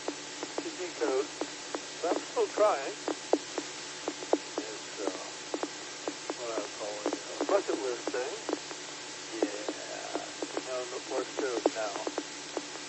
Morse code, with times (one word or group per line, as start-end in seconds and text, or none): none